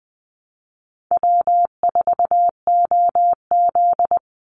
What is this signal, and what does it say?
Morse code, heard as a keyed tone, about 20 wpm, 700 Hz. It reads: W4OZ
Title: USA Call Signs 20wpm